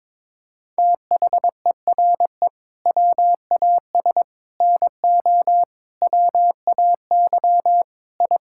Morse code, heard as a keyed tone, about 22 wpm, 700 Hz. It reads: THERE WAS NO WAY I
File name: practice / 22wpm / Sets of 7 Words - 22wpm